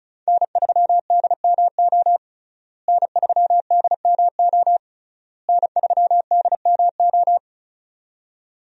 Morse code, heard as a keyed tone, about 34 wpm, 700 Hz. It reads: N3DMO N3DMO N3DMO
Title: USA Call Signs - 5 Characters - Repeated 3x - 35wpm